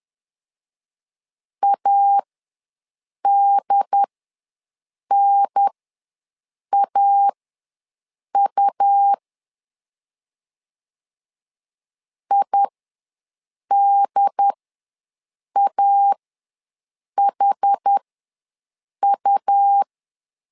Morse code, heard as a keyed tone, about 10 wpm, spaced as for 6 wpm, 800 Hz. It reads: ADNAU IDAHU
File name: LEZIONI / 4 Q ws 10wpm 2306